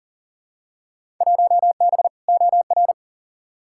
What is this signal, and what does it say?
Morse code, heard as a keyed tone, about 40 wpm, 700 Hz. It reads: 1B OR
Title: Field Day Exchanges - 40wpm